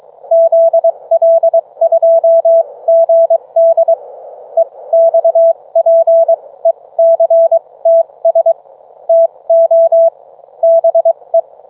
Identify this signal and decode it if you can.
Morse code, heard as a keyed tone, about 23 wpm, spaced as for 19 wpm, 660 Hz. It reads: ZL2GD EXPECTS TO BE